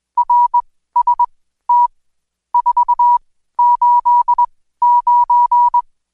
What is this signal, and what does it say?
Morse code, heard as a keyed tone, about 21 wpm, spaced as for 14 wpm, 970 Hz. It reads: RST 489
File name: Code Quick / AudioDownld2 / Seg124-Day29-DayDrill 29